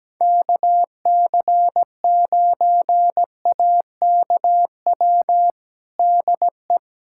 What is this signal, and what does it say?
Morse code, heard as a keyed tone, about 17 wpm, 700 Hz. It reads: KC9AKW DE
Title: USA Callsign DE Callsign 17wpm